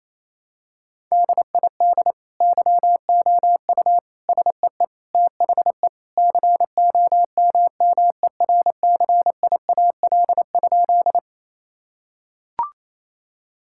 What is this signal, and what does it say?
Morse code, heard as a keyed tone, about 28 wpm, 700 Hz. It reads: DID YOU SEE THE COMMERCIAL?
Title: Sentences from Top 1000 Words - Review Part 5 - 28wpm